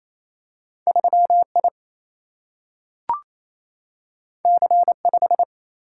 Morse code, heard as a keyed tone, about 28 wpm, 700 Hz. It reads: 3I C5
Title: Single-Letter Single-Number 28wpm